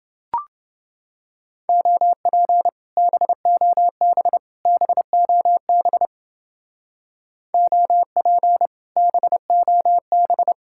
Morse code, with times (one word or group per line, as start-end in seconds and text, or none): 1.7-2.7 OP
3.0-4.4 BOB
4.7-6.1 BOB
7.5-8.7 OP
9.0-10.5 BOB